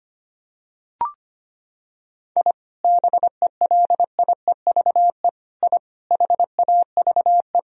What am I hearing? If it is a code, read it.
Morse code, heard as a keyed tone, about 25 wpm, 700 Hz. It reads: I BELIEVE I HAVE
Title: Sentences from Top 1000 Words - Review Part 2 - 25wpm